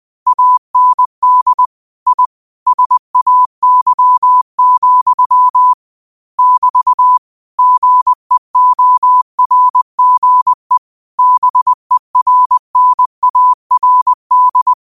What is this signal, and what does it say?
Morse code, heard as a keyed tone, about 20 wpm, 1000 Hz. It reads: AND I SAY, <BT> GEORGE BERNARD
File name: qotdfiles / 20wpm